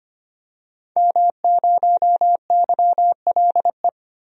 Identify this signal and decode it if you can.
Morse code, heard as a keyed tone, about 25 wpm, 700 Hz. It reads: M0YLE
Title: DXCC-100 Call Signs - 5 Characters - Repeated 2x - 25wpm